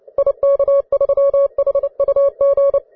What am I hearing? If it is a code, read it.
Morse code, heard as a keyed tone, about 29 wpm, 560 Hz. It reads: IK3HUG
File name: wae / cw advanced / ve3kz cw